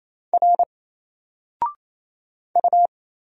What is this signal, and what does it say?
Morse code, heard as a keyed tone, about 27 wpm, 700 Hz. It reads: R U
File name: Single-Letter-Number-28wpm